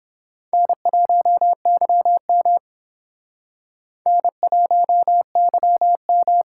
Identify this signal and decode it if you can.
Morse code, heard as a keyed tone, about 26 wpm, 700 Hz. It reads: N1YM N1YM
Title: Speed Racing - USA 1x2 Call Signs 22wpm